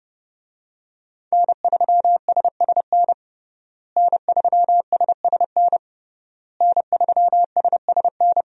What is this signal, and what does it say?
Morse code, heard as a keyed tone, about 30 wpm, 700 Hz. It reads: N3SSN N3SSN N3SSN
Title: USA Call Signs - 5 Characters - Repeated 3x - 30wpm